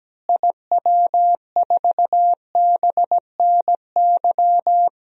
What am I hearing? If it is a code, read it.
Morse code, heard as a keyed tone, about 17 wpm, 700 Hz. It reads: IW4BNY
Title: DXCC-100 Call Signs - 6 Characters - Repeated 3x - 17wpm